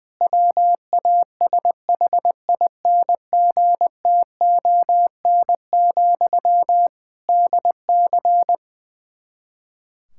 Morse code, heard as a keyed tone, about 20 wpm, 700 Hz. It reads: WASHINGTON, DC
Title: Practice QSO - Exchange 1 - Beginning - 20wpm